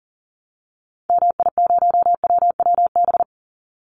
Morse code, heard as a keyed tone, about 40 wpm, 700 Hz. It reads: MI0WWB
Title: Call Signs Repeated 2x - 40wpm